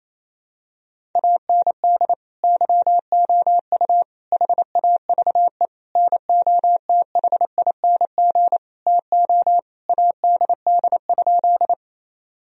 Morse code, heard as a keyed tone, about 28 wpm, 700 Hz. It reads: AND YOU HAVE NOTHING TO ADD?
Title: Sentences from Top 1000 Words - Review Part 1 - 28wpm